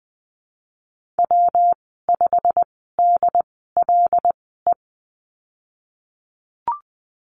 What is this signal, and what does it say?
Morse code, heard as a keyed tone, about 20 wpm, spaced as for 15 wpm, 700 Hz. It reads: W5DLE
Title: 122 Lesson 20g - Number 3 - Call Signs 20wpm